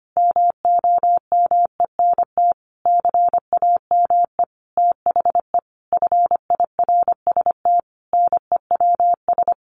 Morse code, heard as a keyed tone, about 25 wpm, 700 Hz. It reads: MOMENT CAME THE FIRST NEWS
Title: Sentences from Top 1500 Words 25wpm